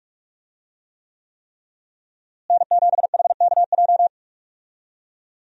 Morse code, heard as a keyed tone, about 45 wpm, 700 Hz. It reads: N7HKJ
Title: USA Call Signs - 5 Characters - Repeated 3x - 45wpm